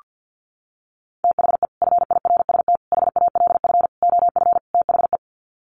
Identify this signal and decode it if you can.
Morse code, heard as a keyed tone, about 48 wpm, 700 Hz. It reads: THE FIRST HALF OF THE
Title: Sets of 5 Words 50wpm